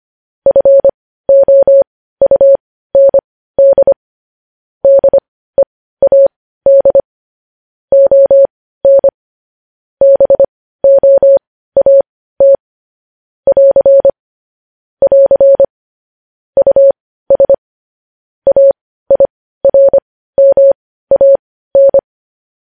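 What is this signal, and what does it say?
Morse code, heard as a keyed tone, about 25 wpm, spaced as for 15 wpm, 550 Hz. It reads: FOUND DEAD ON BOAT <AR> <AR> US AIRMAN
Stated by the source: News headlines sent in Morse code at 15 wpm.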